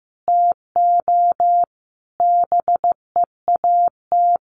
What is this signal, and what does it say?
Morse code, heard as a keyed tone, about 15 wpm, 700 Hz. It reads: TO BEAT
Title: Sentences from Top 1300 Words 15wpm